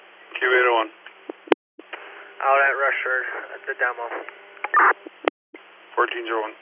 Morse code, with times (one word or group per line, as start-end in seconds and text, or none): none